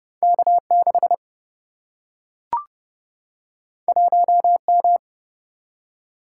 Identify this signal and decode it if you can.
Morse code, heard as a keyed tone, about 32 wpm, 700 Hz. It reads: K 6 1 M
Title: Single-Letter Single-Number 30wpm